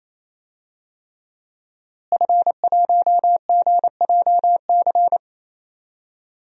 Morse code, heard as a keyed tone, about 28 wpm, 700 Hz. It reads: F1GJC
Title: DXCC-100 Call Signs - 5 Characters - Repeated 3x - 28wpm